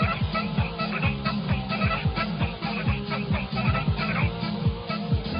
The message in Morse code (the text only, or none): none